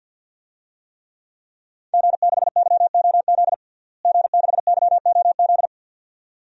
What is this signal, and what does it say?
Morse code, heard as a keyed tone, about 51 wpm, 700 Hz. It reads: G6YOZ G6YOZ
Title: DXCC-100 Call Signs - 5 Characters - Repeated 2x - 50wpm